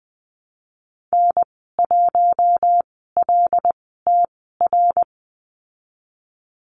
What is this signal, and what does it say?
Morse code, heard as a keyed tone, about 19 wpm, spaced as for 14 wpm, 700 Hz. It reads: N1LTR
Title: 076 Lesson 13g - Letter L - Call Signs 20wpm